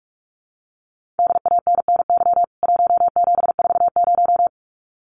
Morse code, heard as a keyed tone, about 44 wpm, 700 Hz. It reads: DANNY 1740
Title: CWT Exchange 45wpm